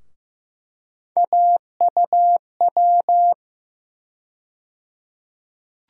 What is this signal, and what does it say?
Morse code, heard as a keyed tone, about 15 wpm, 700 Hz. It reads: AUW
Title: USA Call Sign Postfixes - 3 Letters 15wpm